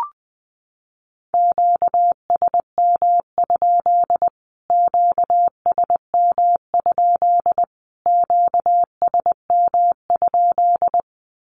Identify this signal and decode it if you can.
Morse code, heard as a keyed tone, about 20 wpm, 700 Hz. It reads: QSM? QSM? QSM?
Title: Q-Codes Repeated 3x 20wpm